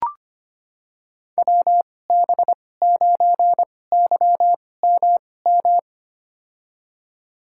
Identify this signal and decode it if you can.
Morse code, heard as a keyed tone, about 25 wpm, spaced as for 18 wpm, 700 Hz. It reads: WB9YMM
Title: USA Call Signs - 6 Characters - 2x Character Spacing - 25wpm